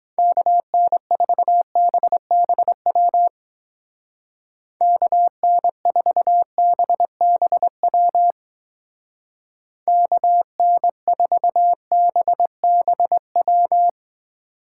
Morse code, heard as a keyed tone, about 21 wpm, 700 Hz. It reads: KN4BBW KN4BBW KN4BBW
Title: Speed Racing - USA 2x3 Call Signs 17wpm